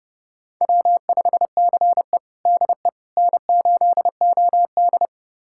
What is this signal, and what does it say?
Morse code, heard as a keyed tone, about 30 wpm, 700 Hz. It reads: W5CE DE N8OD
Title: USA Callsign DE Callsign - 4 Characters - 30wpm